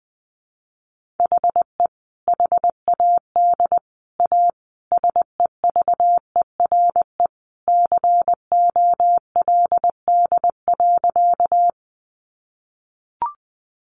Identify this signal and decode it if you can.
Morse code, heard as a keyed tone, about 20 wpm, 700 Hz. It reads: HE HAD A SEVERE COLD.
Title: Sentences from Top 1400 Words 20wpm